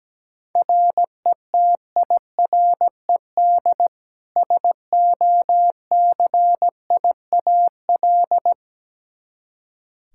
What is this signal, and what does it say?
Morse code, heard as a keyed tone, about 17 wpm, 700 Hz. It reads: RETIRED SOCIAL
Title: Practice QSO - Exchange 3 - More Info - 17wpm